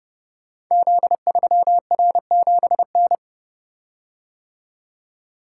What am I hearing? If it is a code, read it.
Morse code, heard as a keyed tone, about 30 wpm, 700 Hz. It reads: Z3R7N
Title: Code Groups - 5 Letter:Numbers - 30wpm